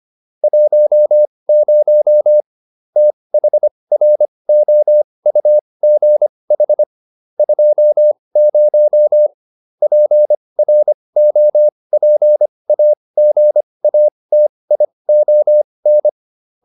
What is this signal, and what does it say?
Morse code, heard as a keyed tone, about 25 wpm, spaced as for 20 wpm, 600 Hz. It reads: 10 THROUGH 20 PROPAGATION